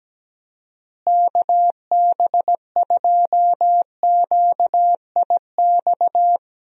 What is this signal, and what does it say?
Morse code, heard as a keyed tone, about 17 wpm, 700 Hz. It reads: KB2QIX
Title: USA Call Signs - 6 Characters - Repeated 3x - 17wpm